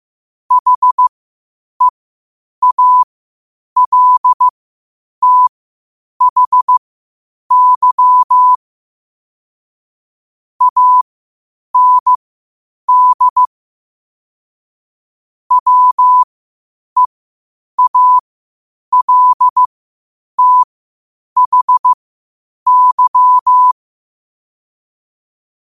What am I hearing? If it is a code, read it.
Morse code, heard as a keyed tone, about 15 wpm, spaced as for 8 wpm, 1000 Hz. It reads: HEALTHY AND WEALTHY
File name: qotdfiles / 8wpm